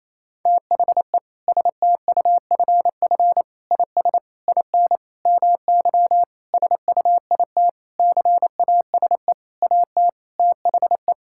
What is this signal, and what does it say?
Morse code, heard as a keyed tone, about 28 wpm, 700 Hz. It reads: THE STUFF IS IN MY SUIT CASE AT THE
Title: Sentences from Top 1600 Words 28wpm